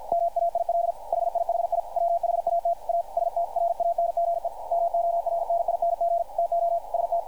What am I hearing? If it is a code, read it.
Morse code, heard as a keyed tone, about 53 wpm, 680 Hz, with radio noise. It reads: MAIL HIHI WHAT A STRANGE MORNING TO ST
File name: archived / CRSnet